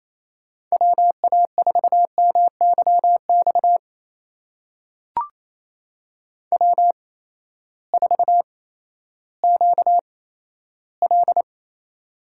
Morse code, heard as a keyed tone, about 27 wpm, spaced as for 14 wpm, 700 Hz. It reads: WA4MYX W4QL
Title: USA Call Signs - 8x Character Spacing - 28wpm